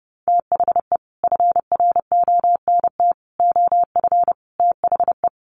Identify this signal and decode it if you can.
Morse code, heard as a keyed tone, about 30 wpm, 700 Hz. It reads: THE FRONT OF THE
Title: Sets of 4 Words 30wpm